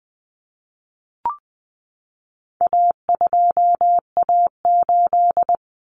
Morse code, heard as a keyed tone, about 20 wpm, 700 Hz. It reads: A2A8
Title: Common Cut Numbers - 4-Digit 20wpm